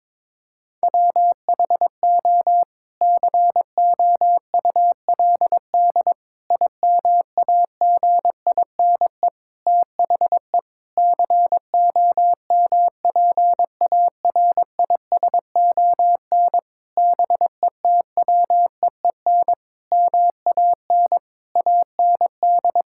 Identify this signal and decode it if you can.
Morse code, heard as a keyed tone, about 22 wpm, 700 Hz. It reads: WHO COULD IMAGINE THE COMPARISON BETWEEN MAN AND
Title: Sentences from Top 1600 Words 22wpm